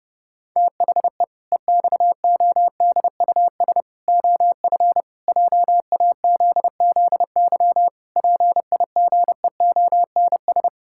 Morse code, heard as a keyed tone, about 32 wpm, 700 Hz. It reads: THE EXODUS OF JAZZY PIGEONS
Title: Pangram Challenge v2 - 30wpm